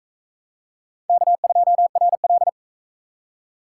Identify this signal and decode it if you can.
Morse code, heard as a keyed tone, about 43 wpm, 700 Hz. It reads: K2RL